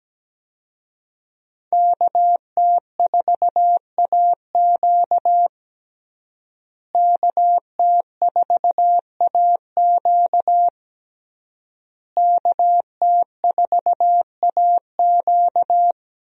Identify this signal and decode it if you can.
Morse code, heard as a keyed tone, about 17 wpm, 700 Hz. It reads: KT4AQ KT4AQ KT4AQ
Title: USA Call Signs - 5 Characters - Repeated 3x - 17wpm